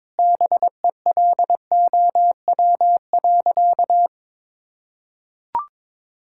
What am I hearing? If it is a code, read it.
Morse code, heard as a keyed tone, about 22 wpm, 700 Hz. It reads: BELOW.
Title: Sentences from Top 1000 Words - Review Part 5 - 22wpm